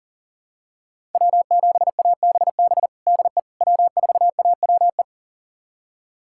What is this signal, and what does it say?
Morse code, heard as a keyed tone, about 40 wpm, 700 Hz. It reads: W7ABB DE W4AWE
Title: USA Callsign DE Callsign - 5 Characters - 40wpm